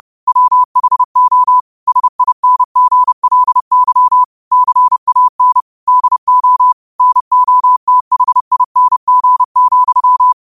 Morse code, heard as a keyed tone, about 30 wpm, 1000 Hz. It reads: WHO SINGLY CAN DO NOTHING,
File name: qotdfiles / 30wpm